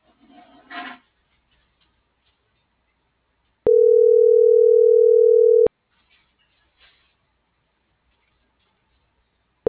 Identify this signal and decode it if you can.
Ringback tone (North America), 440 + 480 Hz.